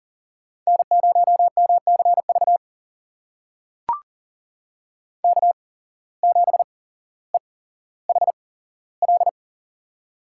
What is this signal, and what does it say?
Morse code, heard as a keyed tone, about 40 wpm, spaced as for 17 wpm, 700 Hz. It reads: N0MCV K7EHL
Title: USA Call Signs - 5 Characters - 8x Character Spacing - 40wpm